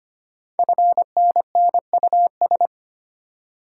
Morse code, heard as a keyed tone, about 25 wpm, 700 Hz. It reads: FNNUS
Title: Code Groups - 5 Letters - 25wpm